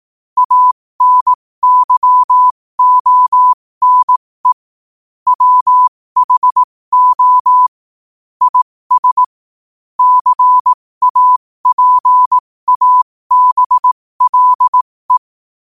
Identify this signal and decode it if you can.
Morse code, heard as a keyed tone, about 18 wpm, spaced as for 15 wpm, 1000 Hz. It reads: ANYONE WHO IS CAPABLE